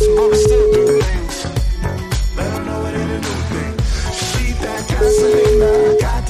Ringback tone (Europe, 425 Hz), with noise.